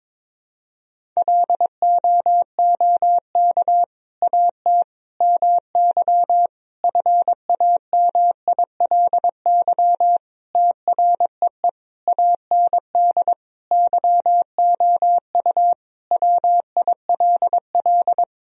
Morse code, heard as a keyed tone, about 22 wpm, 700 Hz. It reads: LOOK AT MY FAMILY TREE AND YOU WILL
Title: Sentences from Top 1000 Words - Review Part 1 - 22wpm